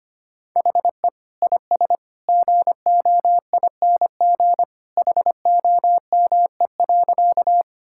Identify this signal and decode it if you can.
Morse code, heard as a keyed tone, about 25 wpm, 700 Hz. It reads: HE IS GOING HOME.
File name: practice / 25wpm / Sentences with 2-Word Phrases - 25wpm